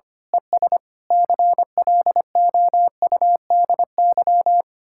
Morse code, heard as a keyed tone, about 24 wpm, 700 Hz. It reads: ES CLOUDY
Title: Speed-Racing Common QSO Phrases - 25wpm